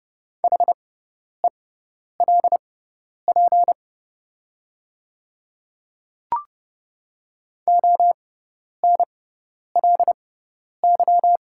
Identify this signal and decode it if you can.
Morse code, heard as a keyed tone, about 30 wpm, spaced as for 10 wpm, 690 Hz. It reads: HELP ONLY
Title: 6x Character Spacing - Top 200 Words - 30wpm